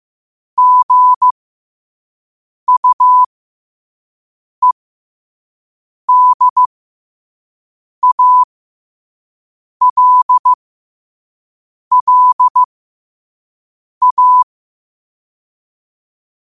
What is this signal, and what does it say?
Morse code, heard as a keyed tone, about 15 wpm, spaced as for 5 wpm, 1000 Hz. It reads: GUEDALLA
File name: qotdfiles / 5wpm